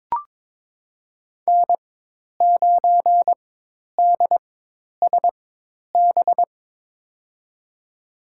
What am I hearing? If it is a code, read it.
Morse code, heard as a keyed tone, about 22 wpm, spaced as for 10 wpm, 700 Hz. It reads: N9DSB
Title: USA Call Signs - 5 Characters - 4x Character Spacing - 22wpm